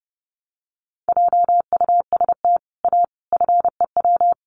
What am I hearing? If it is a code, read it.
Morse code, heard as a keyed tone, about 30 wpm, 700 Hz. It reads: JUST A FEW
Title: Sets of 5 Words Encore 30wpm